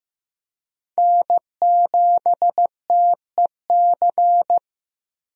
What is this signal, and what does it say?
Morse code, heard as a keyed tone, about 15 wpm, 700 Hz. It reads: N7TEC
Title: USA Call Signs - 5 Characters - 15wpm